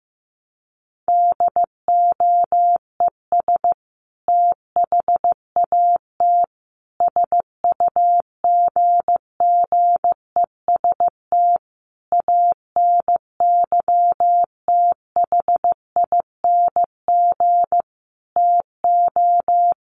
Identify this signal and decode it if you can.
Morse code, heard as a keyed tone, about 14 wpm, 700 Hz. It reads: DOES THAT SUGGEST ANYTHING TO
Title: Sentences from Top 1300 Words 15wpm